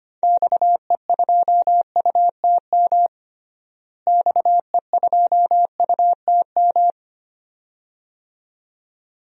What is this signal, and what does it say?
Morse code, heard as a keyed tone, about 25 wpm, 700 Hz. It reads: XE2UTM XE2UTM
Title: DXCC-100 Call Signs - 6 Characters - Repeated 2x - 25wpm